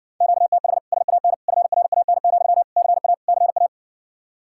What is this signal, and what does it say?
Morse code, heard as a keyed tone, about 61 wpm, 700 Hz. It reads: QTH SAN FRAN, CA CA